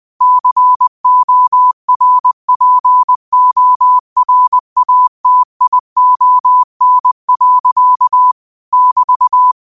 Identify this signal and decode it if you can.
Morse code, heard as a keyed tone, about 20 wpm, 1000 Hz. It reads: CORPORATION. <BT>
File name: qotdfiles / 20wpm